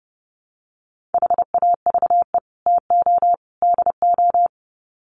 Morse code, heard as a keyed tone, about 30 wpm, 700 Hz. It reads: HAVE TO DO
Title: Sets of 3 Words 30wpm